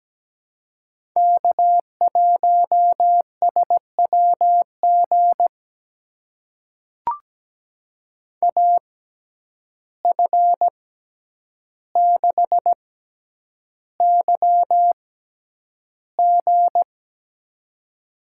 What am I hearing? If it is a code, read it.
Morse code, heard as a keyed tone, about 17 wpm, spaced as for 9 wpm, 700 Hz. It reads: K1SWG AF6YG
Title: USA Call Signs - 5 Characters - 6x Character Spacing - 17wpm